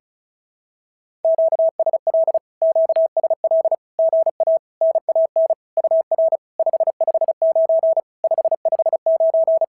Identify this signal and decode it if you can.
Morse code, heard as a keyed tone, about 36 wpm, 650 Hz. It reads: QSL QSL GA NAN UR 559 559
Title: SOTA QSO Practice 35wpm